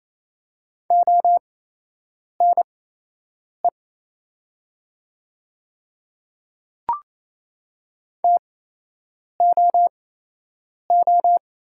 Morse code, heard as a keyed tone, about 28 wpm, spaced as for 8 wpm, 700 Hz. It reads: ONE TOO
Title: 8x Character Spacing - Top 200 Words - 28wpm